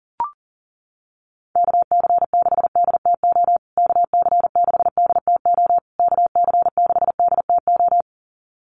Morse code, heard as a keyed tone, about 40 wpm, 700 Hz. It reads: KC6DTO KC6DTO KC6DTO
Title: USA Call Signs Repeated 3x 40wpm